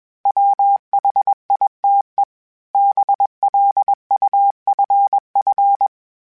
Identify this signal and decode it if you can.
Morse code, heard as a keyed tone, about 20 wpm, 800 Hz. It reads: WHITE BLUFF
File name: fromSMs / FromSMs mp3s / Cities@20